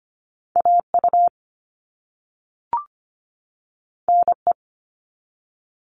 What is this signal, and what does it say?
Morse code, heard as a keyed tone, about 25 wpm, 700 Hz. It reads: AU NE
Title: All Cut Numbers - 2-Digit 25wpm